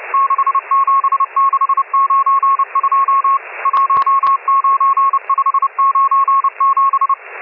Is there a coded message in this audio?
Yes: Morse code from 0.1 to 7.1 s, heard as a keyed tone, about 30 wpm, 1090 Hz, with radio noise. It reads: 67692 19597